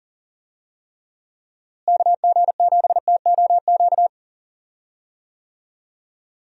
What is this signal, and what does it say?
Morse code, heard as a keyed tone, about 40 wpm, 700 Hz. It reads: KG7TOQ
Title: USA Call Signs - 6 Characters - Repeated 3x - 40wpm